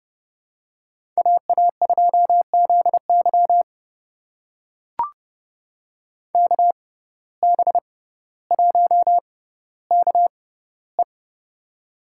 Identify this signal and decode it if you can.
Morse code, heard as a keyed tone, about 32 wpm, spaced as for 16 wpm, 700 Hz. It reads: AA2ZY KB1KE
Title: USA Call Signs - 5 Characters - 6x Character Spacing - 30wpm